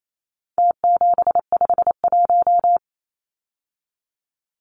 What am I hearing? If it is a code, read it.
Morse code, heard as a keyed tone, about 28 wpm, 700 Hz. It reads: T751
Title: Numbers - Contest Style - 4-Digit 28wpm